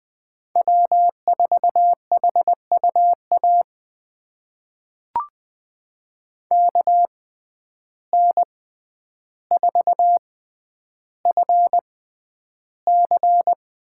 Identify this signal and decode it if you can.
Morse code, heard as a keyed tone, about 20 wpm, spaced as for 10 wpm, 700 Hz. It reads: W4HUA KN4FC
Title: USA Call Signs - 5 Characters - 6x Character Spacing - 20wpm